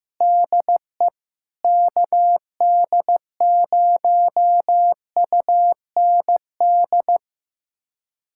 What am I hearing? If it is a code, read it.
Morse code, heard as a keyed tone, about 15 wpm, 700 Hz. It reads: DE KD0UND
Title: USA Callsign DE Callsign - 6 Characters - 15wpm